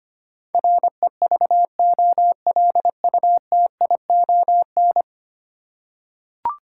Morse code, heard as a keyed tone, about 25 wpm, 700 Hz. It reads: REVOLUTION
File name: practice / 25wpm / Ten Letter Words - 25wpm